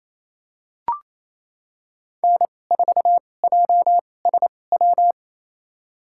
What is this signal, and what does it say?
Morse code, heard as a keyed tone, about 28 wpm, spaced as for 20 wpm, 700 Hz. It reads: N4JSW
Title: USA Call Signs - 5 Characters - 2x Character Spacing - 28wpm